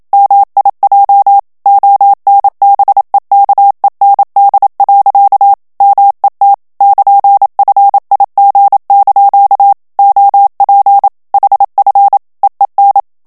Morse code, heard as a keyed tone, about 27 wpm, 800 Hz. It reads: MIJ ONBEKEND. MET <KN>FIG) OP HF EEN
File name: HOLLANDAIS texte clair 1 vitesse 26 mots minute